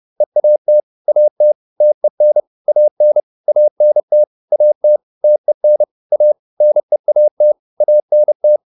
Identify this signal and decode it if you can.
Morse code, heard as a keyed tone, about 30 wpm, 600 Hz. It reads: EAT AT TEN AN ANT AT TEN A NEAT ANT